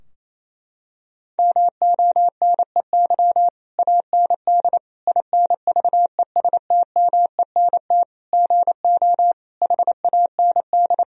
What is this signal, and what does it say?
Morse code, heard as a keyed tone, about 28 wpm, 700 Hz. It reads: MONEY AND INVESTMENT GO HAND
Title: Sentences from Top 1000 Words - Review Part 4 - 28wpm